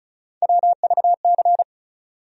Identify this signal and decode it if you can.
Morse code, heard as a keyed tone, about 36 wpm, 700 Hz. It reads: WVC
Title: USA Call Sign Postfixes - 3 Letters 35wpm